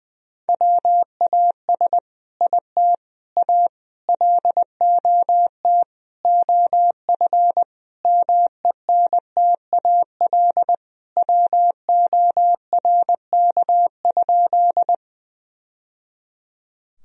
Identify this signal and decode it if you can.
Morse code, heard as a keyed tone, about 21 wpm, 700 Hz. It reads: WAS IT A LOT OF MENTAL WORK?
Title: Sentences from Top 1000 Words - Review Part 4 - 20wpm